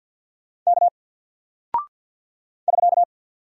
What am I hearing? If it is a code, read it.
Morse code, heard as a keyed tone, about 49 wpm, 700 Hz. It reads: K <SK>
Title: Single Character (Letter:Number:Punctuation:Prosign) 50wpm